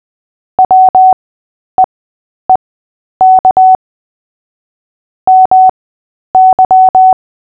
Morse code, heard as a keyed tone, about 20 wpm, spaced as for 10 wpm, 750 Hz. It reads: WEEK MY
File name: Short-Story-5-10wpm